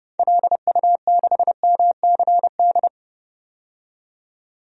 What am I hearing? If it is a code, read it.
Morse code, heard as a keyed tone, about 30 wpm, 700 Hz. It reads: LU6MCD
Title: DXCC-100 Call Signs - 6 Characters - 30wpm